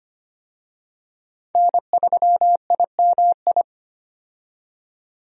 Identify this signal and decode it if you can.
Morse code, heard as a keyed tone, about 25 wpm, 700 Hz. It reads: N3IMI
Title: USA Call Signs - 5 Characters - 25wpm